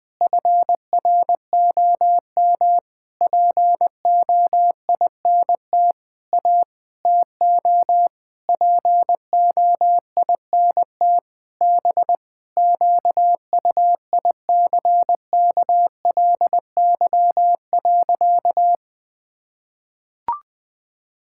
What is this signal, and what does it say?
Morse code, heard as a keyed tone, about 20 wpm, 700 Hz. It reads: FROM POINT A TO POINT B QUICKLY.
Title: Sentences with 6-Word Phrases - 20wpm